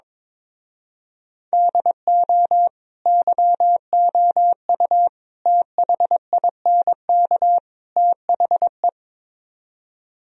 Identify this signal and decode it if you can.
Morse code, heard as a keyed tone, about 22 wpm, 700 Hz. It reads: DO YOU THINK THE
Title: Speed-Racing - Sets of 4 Words 22wpm